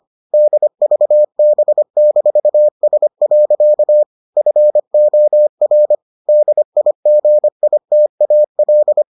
Morse code, heard as a keyed tone, about 25 wpm, 600 Hz. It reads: DVB-S. FOR DIGITAL